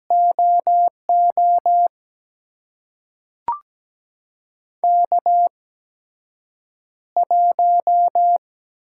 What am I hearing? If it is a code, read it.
Morse code, heard as a keyed tone, about 17 wpm, spaced as for 7 wpm, 700 Hz. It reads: OO K1
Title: USA Call Signs - 5 Characters - 8x Character Spacing - 17wpm